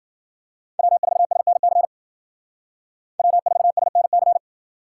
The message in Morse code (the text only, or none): W4SNC W4SNC